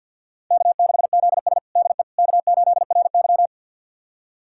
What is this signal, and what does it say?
Morse code, heard as a keyed tone, about 50 wpm, 700 Hz. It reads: K6ZS DE K8RY
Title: USA Callsign DE Callsign - 4 Characters - 50wpm